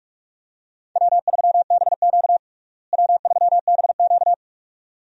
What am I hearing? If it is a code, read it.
Morse code, heard as a keyed tone, about 46 wpm, 700 Hz. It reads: W3BQ W3BQ
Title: USA Call Signs - 4 Characters - Repeated 2x - 45wpm